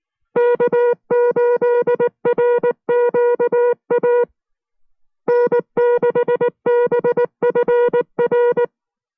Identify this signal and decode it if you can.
Morse code, heard as a keyed tone, about 19 wpm, 480 Hz. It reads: K8RQA N6BFR